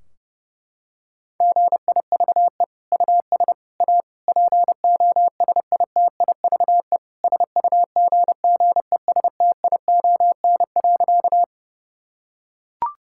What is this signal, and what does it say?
Morse code, heard as a keyed tone, about 30 wpm, 700 Hz. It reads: GIVE US A POSITIVE SUGGESTION.
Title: Sentences from Top 1600 Words 30wpm